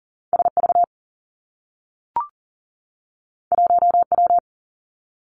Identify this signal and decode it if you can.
Morse code, heard as a keyed tone, about 42 wpm, 700 Hz. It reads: SV 1W
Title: Two Challenging Letter-Number Combinations 40wpm